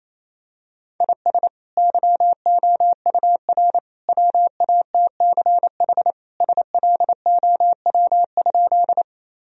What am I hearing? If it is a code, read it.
Morse code, heard as a keyed tone, about 28 wpm, 700 Hz. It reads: IS YOUR WATCH SLOW?
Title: Sentences from Top 1000 Words - Review Part 2 - 28wpm